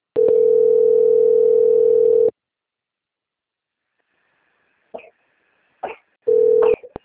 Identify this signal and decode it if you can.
Ringback tone (North America), 440 + 480 Hz.
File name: rg-1600-unknown- - - .32752